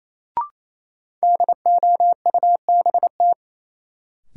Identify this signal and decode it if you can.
Morse code, heard as a keyed tone, about 28 wpm, 700 Hz. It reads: DOUBT
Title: Sets of 5 Words - New Word Drill - 28wpm